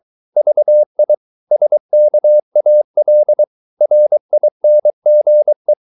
Morse code, heard as a keyed tone, about 23 wpm, 600 Hz. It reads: VI SKAL RINGE
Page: Ferie 23 wpm